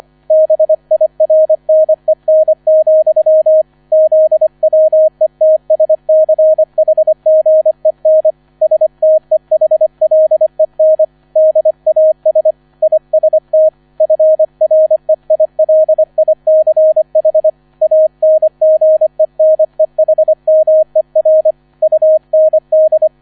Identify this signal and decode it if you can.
Morse code, heard as a keyed tone, about 25 wpm, 630 Hz. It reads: BIRNEN, ZWETSCHGEN STEHLEN DAS IST FREILICH ANGENEHMER UND